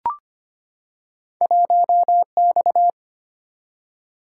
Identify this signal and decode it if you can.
Morse code, heard as a keyed tone, about 25 wpm, 700 Hz. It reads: 1X
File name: Two Letter-Number Combinations - 25wpm